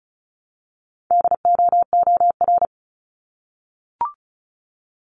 Morse code, heard as a keyed tone, about 36 wpm, 700 Hz. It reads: DOOR
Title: Multi-letter Phonemes Repeated 3x 35wpm